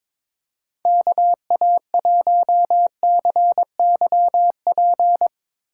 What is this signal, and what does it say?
Morse code, heard as a keyed tone, about 22 wpm, 700 Hz. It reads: KA1CYP
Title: USA Call Signs - 8x Character Spacing - 22wpm